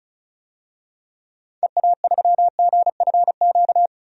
Morse code, heard as a keyed tone, about 34 wpm, 700 Hz. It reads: EA3GFQ
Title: DXCC-100 Call Signs - 6 Characters - 35wpm